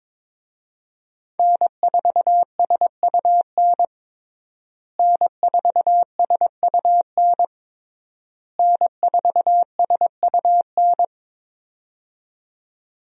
Morse code, heard as a keyed tone, about 22 wpm, 700 Hz. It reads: N4SUN N4SUN N4SUN
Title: USA Call Signs - 5 Characters - Repeated 3x - 22wpm